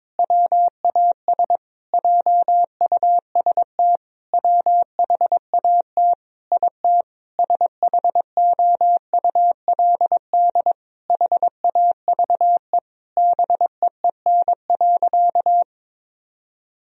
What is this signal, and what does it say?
Morse code, heard as a keyed tone, about 22 wpm, 700 Hz. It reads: WAS JUST WHAT IT SHOULD HAVE BEEN.
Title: Sentences from Top 1000 Words - Review Part 4 - 22wpm